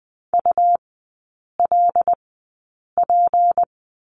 Morse code, heard as a keyed tone, about 20 wpm, spaced as for 8 wpm, 700 Hz. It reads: ULP
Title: 159 Lesson 26e - Letter g - Three Characters 20wpm